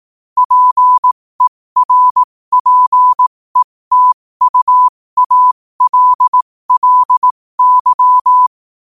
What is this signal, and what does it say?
Morse code, heard as a keyed tone, about 18 wpm, spaced as for 15 wpm, 1000 Hz. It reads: PERPETUALLY
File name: qotdfiles / 15wpm